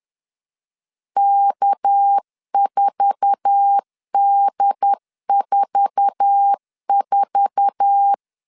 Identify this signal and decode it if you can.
Morse code, heard as a keyed tone, about 10 wpm, 800 Hz. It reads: K4D44
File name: LEZIONI / 8 YJ45 / 8 YJ4 10wpm 2633